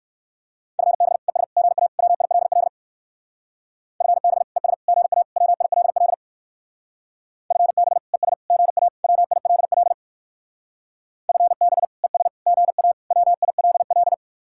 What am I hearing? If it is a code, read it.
Morse code, heard as a keyed tone, about 57 wpm, 700 Hz. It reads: FB ES GA WILL FB ES GA WILL FB ES GA WILL FB ES GA WILL